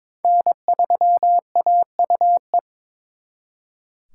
Morse code, heard as a keyed tone, about 26 wpm, 700 Hz. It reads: N3AUE
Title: Speed Racing - USA Call Signs 22wpm